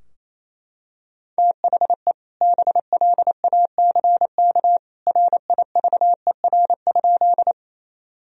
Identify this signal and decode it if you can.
Morse code, heard as a keyed tone, about 28 wpm, 700 Hz. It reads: THE BLACK RIVER?
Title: Sentences from Top 1000 Words - Review Part 2 - 28wpm